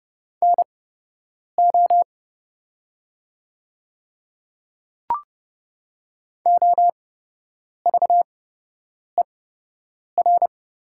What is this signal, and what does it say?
Morse code, heard as a keyed tone, about 30 wpm, spaced as for 8 wpm, 700 Hz. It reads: NO OVER
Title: 8x Character Spacing - Top 100 Words - 30wpm